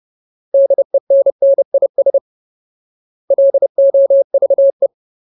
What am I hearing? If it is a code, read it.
Morse code, heard as a keyed tone, about 30 wpm, 550 Hz. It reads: DENNIS LOVE
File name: Ultimate Word List / 30WPM Ultimate